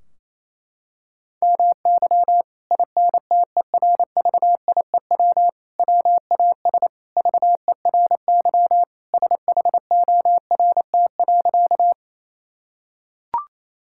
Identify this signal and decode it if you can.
Morse code, heard as a keyed tone, about 28 wpm, 700 Hz. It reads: MY INTERVIEW WAS VERY SHORT.
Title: Sentences from Top 1600 Words 28wpm